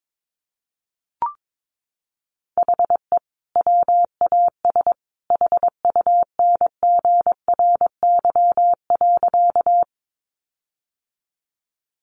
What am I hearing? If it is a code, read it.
Morse code, heard as a keyed tone, about 22 wpm, 700 Hz. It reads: HE WAS HUNGRY.
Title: Sentences from Top 1400 Words 22wpm